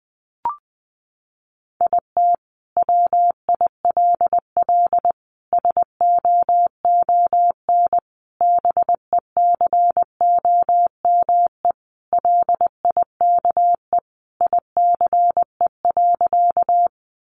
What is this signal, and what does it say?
Morse code, heard as a keyed tone, about 20 wpm, 700 Hz. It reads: IT WILL SOON BECOME LIKE ICE.
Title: Sentences from Top 1000 Words 20wpm